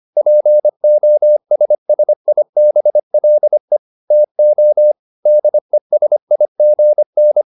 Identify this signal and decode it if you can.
Morse code, heard as a keyed tone, about 25 wpm, 600 Hz. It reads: POSSIBLE TO DESIGN
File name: SS315 25